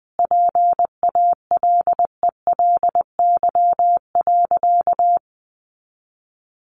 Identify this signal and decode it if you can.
Morse code, heard as a keyed tone, about 20 wpm, 700 Hz. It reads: PALELY.
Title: A Sound Of Thunder 20wpm